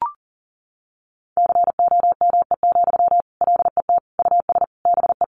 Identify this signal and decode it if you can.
Morse code, heard as a keyed tone, about 40 wpm, 700 Hz. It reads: COME, LET US BE
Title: Sentences from Top 300 Words 40wpm